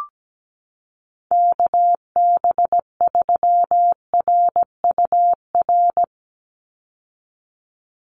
Morse code, heard as a keyed tone, about 17 wpm, 700 Hz. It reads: KB3RUR
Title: USA Call Signs 17wpm